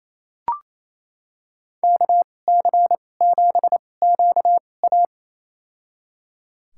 Morse code, heard as a keyed tone, about 28 wpm, spaced as for 20 wpm, 700 Hz. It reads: KC7QA
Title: USA Call Signs - 5 Characters - 2x Character Spacing - 28wpm